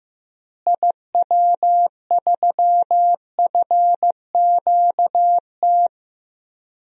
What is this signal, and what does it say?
Morse code, heard as a keyed tone, about 15 wpm, 700 Hz. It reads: IW3FQT
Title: DXCC-100 Call Signs - 6 Characters - Repeated 3x - 15wpm